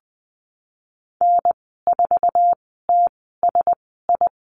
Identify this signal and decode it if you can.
Morse code, heard as a keyed tone, about 19 wpm, spaced as for 14 wpm, 700 Hz. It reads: N4TSI
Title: 048 Lesson 9g - Number 4 - Call Signs 20wpm